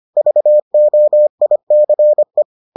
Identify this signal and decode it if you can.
Morse code, heard as a keyed tone, about 25 wpm, 600 Hz. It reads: VOICE